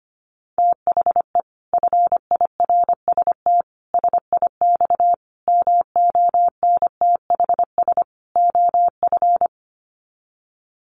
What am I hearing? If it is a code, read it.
Morse code, heard as a keyed tone, about 25 wpm, 700 Hz. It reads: THE FIRST SIX MONTHS OF
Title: Sets of 5 Words Encore2 25wpm